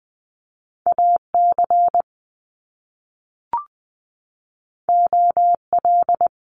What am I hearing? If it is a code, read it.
Morse code, heard as a keyed tone, about 20 wpm, 700 Hz. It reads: AC OL
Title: Top Two Letter Combinations in Words 20wpm